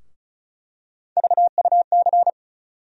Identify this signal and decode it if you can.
Morse code, heard as a keyed tone, about 36 wpm, 700 Hz. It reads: VUC
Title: USA Call Sign Postfixes - 3 Letters 35wpm